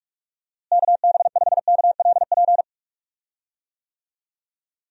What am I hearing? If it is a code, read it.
Morse code, heard as a keyed tone, about 46 wpm, 700 Hz. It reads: KB5KLP
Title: USA Call Signs - 6 Characters - Repeated 2x - 45wpm